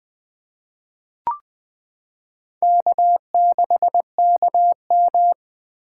Morse code, heard as a keyed tone, about 20 wpm, 700 Hz. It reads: K6KM
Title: 2023 SOTA Chaser Call Signs - 20wpm